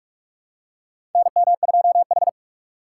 Morse code, heard as a keyed tone, about 45 wpm, 700 Hz. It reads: NM2H
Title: USA Call Signs - 4 Characters - 4x Character Spacing - 45wpm